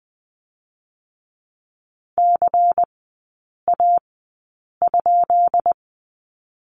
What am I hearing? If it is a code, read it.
Morse code, heard as a keyed tone, about 20 wpm, spaced as for 8 wpm, 700 Hz. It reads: CA?
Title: 218 Lesson 35e - Number 0 - Three Characters 20wpm